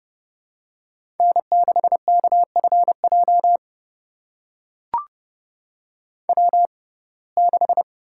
Morse code, heard as a keyed tone, about 30 wpm, spaced as for 22 wpm, 700 Hz. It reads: N6KFJ W6
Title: USA Call Signs - 5 Characters - 6x Character Spacing - 30wpm